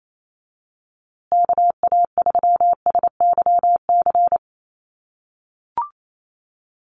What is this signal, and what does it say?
Morse code, heard as a keyed tone, about 28 wpm, 700 Hz. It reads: KA3SYC
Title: USA Call Signs Repeated 2x 28wpm